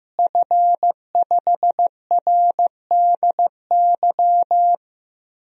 Morse code, heard as a keyed tone, about 15 wpm, 700 Hz. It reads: F5RDY
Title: DXCC-100 Call Signs - 5 Characters - Repeated 3x - 15wpm